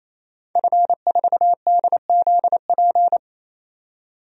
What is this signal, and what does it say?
Morse code, heard as a keyed tone, about 28 wpm, 700 Hz. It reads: F4DZP
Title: DXCC-100 Call Signs - 5 Characters - 28wpm